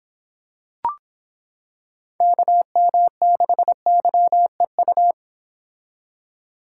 Morse code, heard as a keyed tone, about 26 wpm, 700 Hz. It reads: KM6YEU